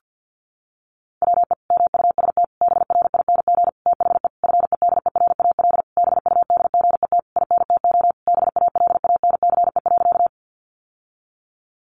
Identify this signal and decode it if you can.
Morse code, heard as a keyed tone, about 50 wpm, 700 Hz. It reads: WE MUST BRING THE FEDERAL BUDGET INTO BALANCE.
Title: Sentences from Top 1200 Words 50wpm